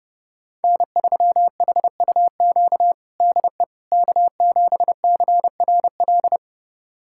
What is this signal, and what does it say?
Morse code, heard as a keyed tone, about 29 wpm, 700 Hz. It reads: N3HUQ DE K7CRL
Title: USA Callsign DE Callsign - 5 Characters - 30wpm